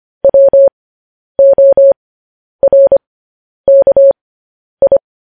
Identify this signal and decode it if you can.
Morse code, heard as a keyed tone, about 25 wpm, spaced as for 10 wpm, 550 Hz. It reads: WORKI
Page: news-headlines-in-morse-code-at-10-wpm